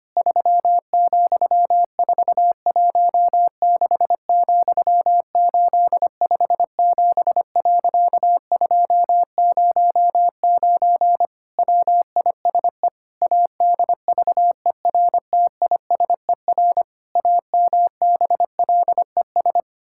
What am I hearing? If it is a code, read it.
Morse code, heard as a keyed tone, about 25 wpm, 700 Hz. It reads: 3,416,857.209 WISE ADVERTISER AMBLES
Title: Pangram Challenge v2 - 25wpm